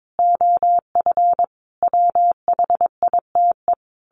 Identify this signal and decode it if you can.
Morse code, heard as a keyed tone, about 22 wpm, 700 Hz. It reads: OF WHITE
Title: A Sound Of Thunder 22wpm